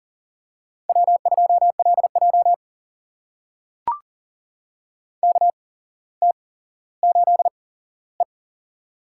Morse code, heard as a keyed tone, about 40 wpm, spaced as for 17 wpm, 700 Hz. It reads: W2LJ KT8E
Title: USA Call Signs - 4 Characters - 8x Character Spacing - 40wpm